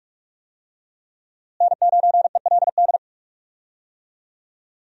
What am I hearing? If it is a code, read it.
Morse code, heard as a keyed tone, about 46 wpm, 700 Hz. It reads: N9ELD
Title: USA Call Signs - 5 Characters - Repeated 2x - 45wpm